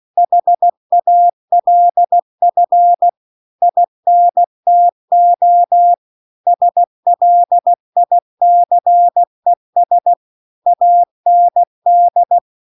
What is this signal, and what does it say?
Morse code, heard as a keyed tone, about 16 wpm, 700 Hz. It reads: HALF INTO SLICES AND
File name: CW-The-Big-Twoheaded-River-2-16WPM